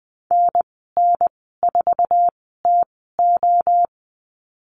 Morse code, heard as a keyed tone, about 20 wpm, spaced as for 14 wpm, 700 Hz. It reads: NN4TO
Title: 062 Lesson 11g - Letter h - Call Signs 20wpm